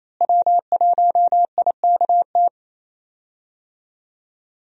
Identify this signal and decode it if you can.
Morse code, heard as a keyed tone, about 28 wpm, 700 Hz. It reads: W1IKT
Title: USA Call Signs - 5 Characters - Repeated 2x - 28wpm